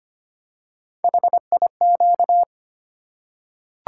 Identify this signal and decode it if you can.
Morse code, heard as a keyed tone, about 25 wpm, 700 Hz. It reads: HIQ
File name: Three Letter-Number Combinations - 25wpm